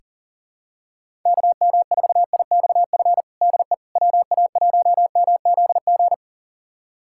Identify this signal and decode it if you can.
Morse code, heard as a keyed tone, about 40 wpm, 700 Hz. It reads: KM4IXF DE WA1MZG
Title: USA Callsign DE Callsign 40wpm